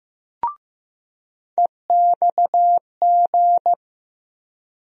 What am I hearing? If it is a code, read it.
Morse code, heard as a keyed tone, about 15 wpm, 700 Hz. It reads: EXG
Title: USA Call Sign Postfixes - 3 Letters 15wpm